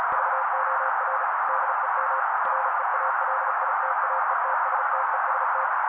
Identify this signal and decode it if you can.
Morse code, heard as a keyed tone, about 35 wpm, 530 Hz. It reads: VOR DEM REALITAETSTEST
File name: daily / 35wpm noise